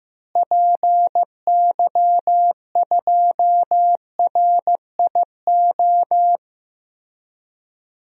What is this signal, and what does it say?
Morse code, heard as a keyed tone, about 15 wpm, 700 Hz. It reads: PY2RIO
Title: DXCC-100 Call Signs - 6 Characters - Repeated 3x - 15wpm